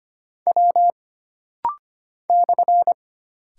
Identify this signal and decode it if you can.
Morse code, heard as a keyed tone, about 25 wpm, 700 Hz. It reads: W /
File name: Single Character (Letter:Number:Punctuation:Prosign) - Rapid-Fire - 25wpm